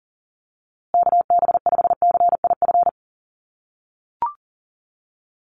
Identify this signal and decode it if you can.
Morse code, heard as a keyed tone, about 40 wpm, 700 Hz. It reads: KB5CIF
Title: USA Call Signs Repeated 3x 40wpm